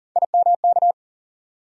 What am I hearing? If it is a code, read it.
Morse code, heard as a keyed tone, about 40 wpm, 700 Hz. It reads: IMK
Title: USA Call Sign Postfixes - 3 Letters 40wpm